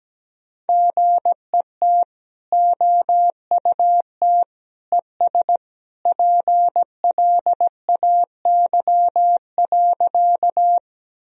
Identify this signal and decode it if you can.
Morse code, heard as a keyed tone, about 17 wpm, 700 Hz. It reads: GET OUT ES PLAY.
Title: Sentences from Top 200 Words and Abbreviations 17wpm